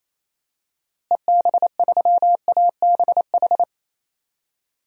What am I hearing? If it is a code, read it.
Morse code, heard as a keyed tone, about 28 wpm, 700 Hz. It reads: EB3ABH
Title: DXCC-100 Call Signs - 6 Characters - Repeated 3x - 28wpm